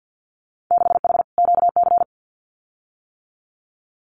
Morse code, heard as a keyed tone, about 50 wpm, 700 Hz. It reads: 6H QC
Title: Winter Field Day Exchanges Encore 50wpm